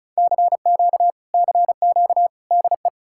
Morse code, heard as a keyed tone, about 47 wpm, 700 Hz. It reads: CQ CQ DE